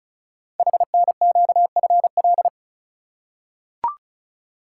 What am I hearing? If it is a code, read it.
Morse code, heard as a keyed tone, about 36 wpm, 690 Hz. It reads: HNQFL
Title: Code Groups - 5 Letters - 35wpm